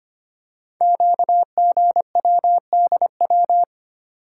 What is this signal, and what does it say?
Morse code, heard as a keyed tone, about 25 wpm, 700 Hz. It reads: QGWDW
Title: Code Groups - 5 Letters - 25wpm